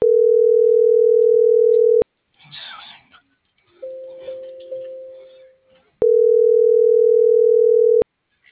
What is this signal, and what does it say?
Ringback tone (North America), 440 + 480 Hz.